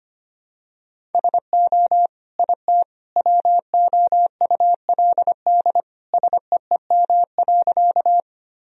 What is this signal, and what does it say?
Morse code, heard as a keyed tone, about 25 wpm, 700 Hz. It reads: SO IT WOULD SEEM.
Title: Sentences from Top 1000 Words - Review Part 1 - 25wpm